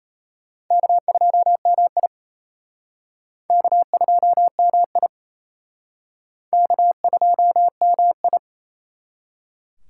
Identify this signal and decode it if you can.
Morse code, heard as a keyed tone, about 32 wpm, 700 Hz. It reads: K2MI K2MI K2MI
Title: Speed Racing - USA 1x2 Call Signs 28wpm